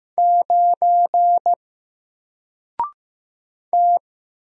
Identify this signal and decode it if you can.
Morse code, heard as a keyed tone, about 15 wpm, 700 Hz. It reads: 9 T
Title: Single Character (Letter:Number:Punctuation:Prosign) 15wpm